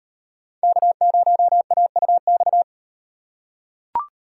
Morse code, heard as a keyed tone, about 36 wpm, 700 Hz. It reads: K0AUX